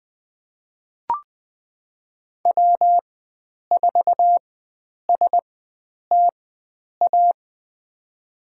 Morse code, heard as a keyed tone, about 21 wpm, spaced as for 9 wpm, 700 Hz. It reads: W4STA
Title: USA Call Signs - 5 Characters - 4x Character Spacing - 20wpm